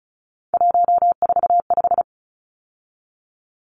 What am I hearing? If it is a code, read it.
Morse code, heard as a keyed tone, about 34 wpm, 700 Hz. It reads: 145
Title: Numbers - 3-Digit 35wpm